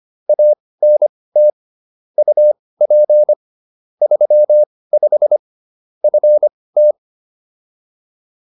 Morse code, heard as a keyed tone, about 25 wpm, spaced as for 18 wpm, 600 Hz. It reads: ANT UP 35 FT